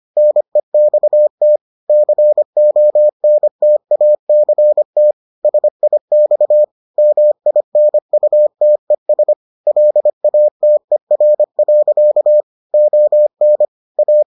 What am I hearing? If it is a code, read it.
Morse code, heard as a keyed tone, about 25 wpm, 600 Hz. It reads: NEXT CONTACT SIX MINUTES LATER. ON A